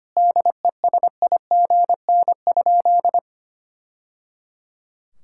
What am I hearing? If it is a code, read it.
Morse code, heard as a keyed tone, about 25 wpm, 700 Hz. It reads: DESIGN?
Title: Sentences from Top 1000 Words - Review Part 5 - 25wpm